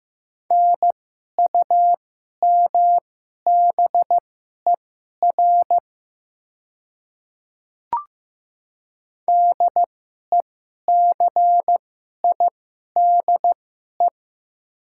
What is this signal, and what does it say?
Morse code, heard as a keyed tone, about 15 wpm, spaced as for 11 wpm, 690 Hz. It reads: NUMBER DECIDE
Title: 2x Character Spacing - Top 500 Words - 15wpm